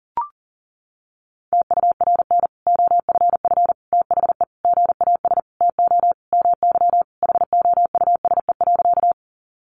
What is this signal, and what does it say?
Morse code, heard as a keyed tone, about 40 wpm, 700 Hz. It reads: TURN OFF THE GAS TO MY HOUSE.
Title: Sentences from Top 1100 Words 40wpm